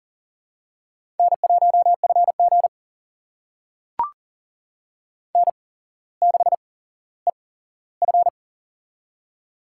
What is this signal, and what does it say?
Morse code, heard as a keyed tone, about 40 wpm, spaced as for 17 wpm, 690 Hz. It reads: N1FG N6EF
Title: USA Call Signs - 4 Characters - 8x Character Spacing - 40wpm